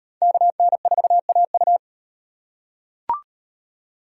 Morse code, heard as a keyed tone, about 38 wpm, 700 Hz. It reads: KN4AU